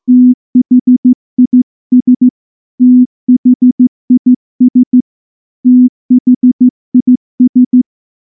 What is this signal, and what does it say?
Morse code, heard as a keyed tone, about 14 wpm, 250 Hz. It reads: THIS THIS THIS
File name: morse-training-15WPM